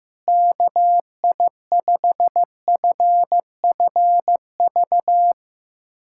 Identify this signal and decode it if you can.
Morse code, heard as a keyed tone, about 15 wpm, 700 Hz. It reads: KI5FFV
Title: USA Callsign DE Callsign 15wpm